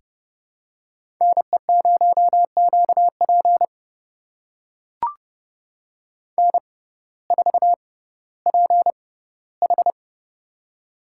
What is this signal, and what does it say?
Morse code, heard as a keyed tone, about 30 wpm, spaced as for 16 wpm, 700 Hz. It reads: NE0QP N4PH
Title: USA Call Signs - 6x Character Spacing - 30wpm